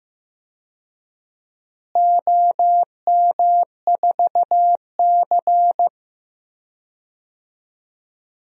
Morse code, heard as a keyed tone, about 15 wpm, 700 Hz. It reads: OM4C
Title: DXCC-100 Call Signs - 4 Characters - Repeated 3x - 15wpm